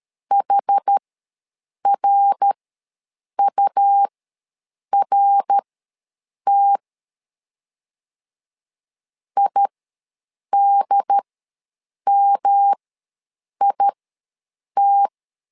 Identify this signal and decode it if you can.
Morse code, heard as a keyed tone, about 12 wpm, spaced as for 7 wpm, 800 Hz. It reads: HRURT IDMIT
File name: LEZIONI / 3 MWRO / 3 MWR ws 12wpm 1809